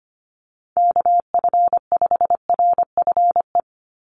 Morse code, heard as a keyed tone, about 25 wpm, 700 Hz. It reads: KF5RFE
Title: USA Call Signs 25wpm